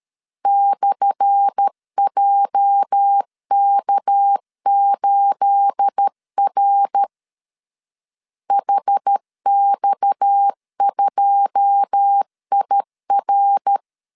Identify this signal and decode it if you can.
Morse code, heard as a keyed tone, about 13 wpm, 800 Hz. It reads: /JK8R HX2IR
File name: LEZIONI / 10 SIGNS / 10A SLASH 12wpm 2314